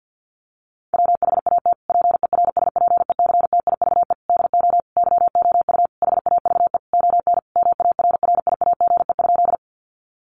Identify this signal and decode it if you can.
Morse code, heard as a keyed tone, about 50 wpm, 700 Hz. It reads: WHAT PERSPECTIVE DO YOU HAVE ON MARRIAGE?
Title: Sentences from Top 1200 Words 50wpm